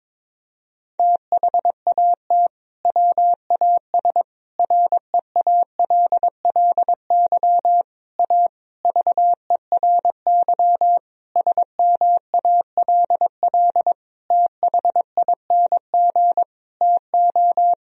Morse code, heard as a keyed tone, about 22 wpm, 700 Hz. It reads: THAT WAS REALLY A VERY SMALL THING TO
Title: Sentences from Top 1000 Words - Review Part 1 - 22wpm